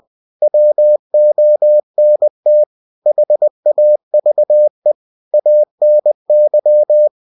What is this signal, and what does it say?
Morse code, heard as a keyed tone, about 20 wpm, 600 Hz. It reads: WONT HAVE ANY
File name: cw-books / winnie-the-pooh / winnie-20wpm-page-0016